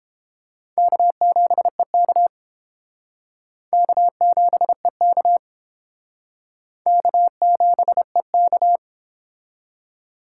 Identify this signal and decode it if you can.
Morse code, heard as a keyed tone, about 28 wpm, 700 Hz. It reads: K7EK K7EK K7EK